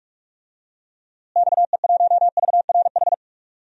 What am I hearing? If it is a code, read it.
Morse code, heard as a keyed tone, about 46 wpm, 700 Hz. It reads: XE1VRH
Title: DXCC-100 Call Signs - 6 Characters - Repeated 3x - 45wpm